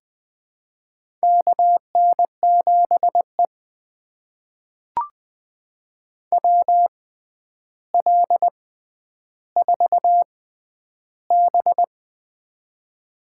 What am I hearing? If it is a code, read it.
Morse code, heard as a keyed tone, about 20 wpm, spaced as for 10 wpm, 700 Hz. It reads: KN7E WL4B
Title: USA Call Signs - 4 Characters - 6x Character Spacing - 20wpm